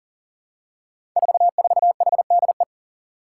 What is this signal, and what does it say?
Morse code, heard as a keyed tone, about 38 wpm, 700 Hz. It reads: 44HDE
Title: Five Challenging Letter-Number Combinations 40wpm